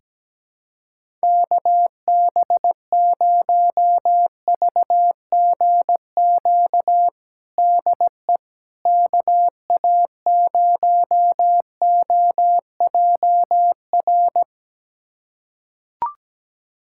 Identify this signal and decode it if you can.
Morse code, heard as a keyed tone, about 17 wpm, 700 Hz. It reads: KB0VGQ DE KA0OJR
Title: USA Callsign DE Callsign 17wpm